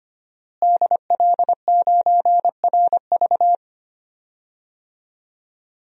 Morse code, heard as a keyed tone, about 25 wpm, 700 Hz. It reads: DL9RV
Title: DXCC-100 Call Signs - 5 Characters - Repeated 2x - 25wpm